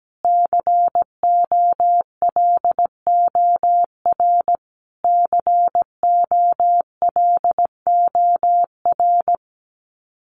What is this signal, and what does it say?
Morse code, heard as a keyed tone, about 17 wpm, 700 Hz. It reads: COLOR COLOR
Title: Top 500 Words Repeated 17wpm